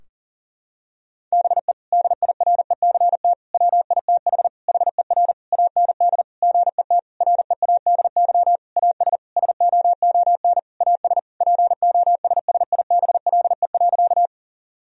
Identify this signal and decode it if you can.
Morse code, heard as a keyed tone, about 40 wpm, 700 Hz. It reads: BE DIRECT WITH HER AND GET READY AS SOON AS POSSIBLE.
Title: Sentences from Top 1000 Words - Review Part 2 - 40wpm